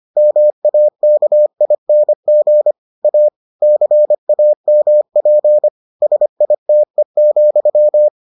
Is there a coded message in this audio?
Yes: Morse code from 0.2 to 8.1 s, heard as a keyed tone, about 25 wpm, 600 Hz. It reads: MAKING A CAMP SITE,